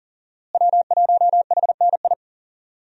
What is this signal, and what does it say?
Morse code, heard as a keyed tone, about 37 wpm, 700 Hz. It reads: W1HNI
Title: USA Call Signs - 5 Characters - Repeated 2x - 40wpm